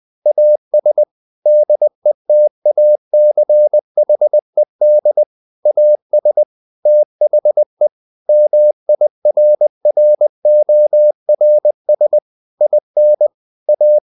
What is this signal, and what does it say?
Morse code, heard as a keyed tone, about 20 wpm, 600 Hz. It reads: AS DETACHED AS THE MIRRORS IN A
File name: the old man and the sea part 2 20wpm